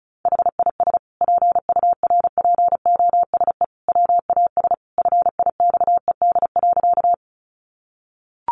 Morse code, heard as a keyed tone, about 34 wpm, 700 Hz. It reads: HIS PURPOSE WAS FIXED.
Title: Sentences from Top 1100 Words 35wpm